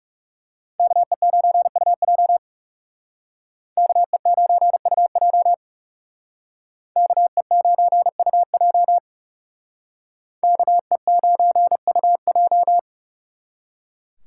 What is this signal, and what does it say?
Morse code, heard as a keyed tone, about 39 wpm, 700 Hz. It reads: KE9UJ KE9UJ KE9UJ KE9UJ